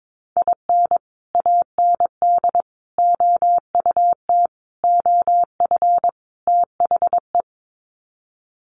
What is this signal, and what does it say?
Morse code, heard as a keyed tone, about 22 wpm, 700 Hz. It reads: IN AND OUT OF THE
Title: Sets of 5 Words 22wpm